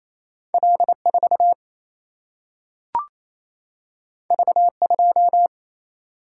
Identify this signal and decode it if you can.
Morse code, heard as a keyed tone, about 28 wpm, 700 Hz. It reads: L4 V2
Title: Single-Letter Single-Number 28wpm